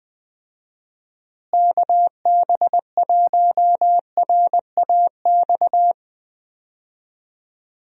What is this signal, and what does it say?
Morse code, heard as a keyed tone, about 20 wpm, 700 Hz. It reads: KB1RAX
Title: USA Call Signs - 6 Characters - Repeated 2x - 20wpm